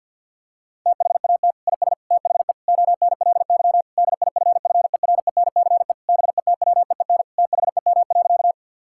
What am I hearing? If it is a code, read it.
Morse code, heard as a keyed tone, about 47 wpm, 700 Hz. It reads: THAT IS THE ONLY DIFFERENCE BETWEEN THEM.
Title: Sentences from Top 1000 Words - Review Part 5 - 50wpm